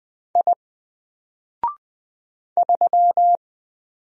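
Morse code, heard as a keyed tone, about 20 wpm, 700 Hz. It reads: I 3
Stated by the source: Single letter and number Morse drill at 20 wpm, from N4LCD.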